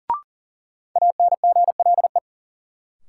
Morse code, heard as a keyed tone, about 38 wpm, 700 Hz. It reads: ANGLE
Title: Sentences from Top 1500 Words - New Word Drill Alt - 40wpm